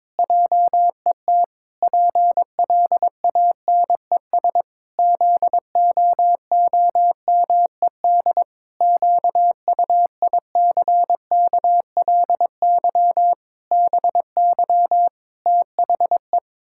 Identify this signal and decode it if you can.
Morse code, heard as a keyed tone, about 21 wpm, 700 Hz. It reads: JET PLANES ZOOMED QUICKLY BY THE
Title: Pangram Challenge v2 - 22wpm